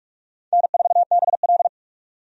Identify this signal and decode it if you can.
Morse code, heard as a keyed tone, about 46 wpm, 700 Hz. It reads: N4BL
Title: USA Call Signs - 4 Characters - 45wpm